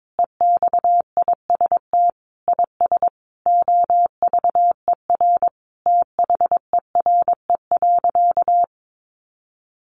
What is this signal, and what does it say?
Morse code, heard as a keyed tone, about 22 wpm, 700 Hz. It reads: EXIST IS OVER THERE.
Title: Sentences from Top 1100 Words 22wpm